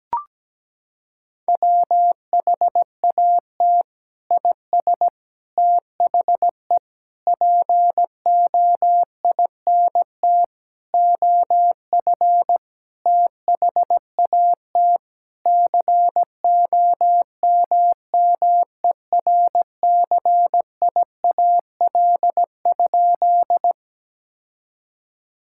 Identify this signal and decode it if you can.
Morse code, heard as a keyed tone, about 17 wpm, 700 Hz. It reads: WHAT IS THE POINT OF THAT COMMERCIAL?
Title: Sentences from Top 1000 Words - Review Part 5 - 17wpm